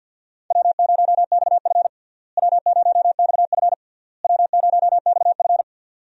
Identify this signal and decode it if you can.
Morse code, heard as a keyed tone, about 52 wpm, 700 Hz. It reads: W0XF W0XF W0XF
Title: USA Call Signs - 4 Characters - Repeated 3x - 50wpm